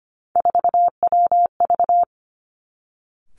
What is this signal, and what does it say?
Morse code, heard as a keyed tone, about 25 wpm, 700 Hz. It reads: 4WV
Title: Three Challenging Letter-Number Combinations 25wpm